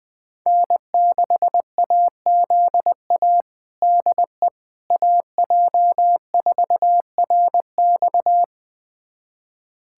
Morse code, heard as a keyed tone, about 20 wpm, 700 Hz. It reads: N6AZA DE AJ4RX
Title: USA Callsign DE Callsign - 5 Characters - 20wpm